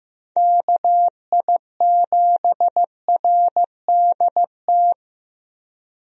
Morse code, heard as a keyed tone, about 15 wpm, 700 Hz. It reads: KI7RDT
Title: USA Call Signs - 6 Characters - Repeated 2x - 15wpm